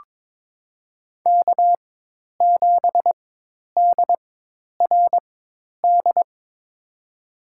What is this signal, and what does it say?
Morse code, heard as a keyed tone, about 22 wpm, spaced as for 10 wpm, 700 Hz. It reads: K7DRD
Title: USA Call Signs - 4x Character Spacing - 22wpm